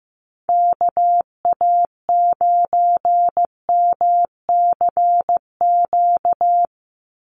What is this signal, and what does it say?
Morse code, heard as a keyed tone, about 15 wpm, 700 Hz. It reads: KA9MCQ
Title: USA Call Signs 15wpm